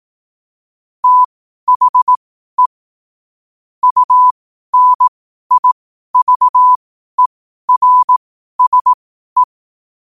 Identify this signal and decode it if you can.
Morse code, heard as a keyed tone, about 18 wpm, spaced as for 12 wpm, 1000 Hz. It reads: THE UNIVERSE
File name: qotdfiles / 12wpm